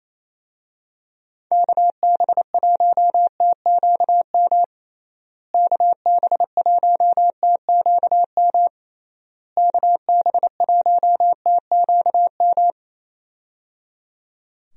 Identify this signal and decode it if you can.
Morse code, heard as a keyed tone, about 28 wpm, 700 Hz. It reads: KB1TQM KB1TQM KB1TQM
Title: USA Call Signs - 6 Characters - Repeated 3x - 28wpm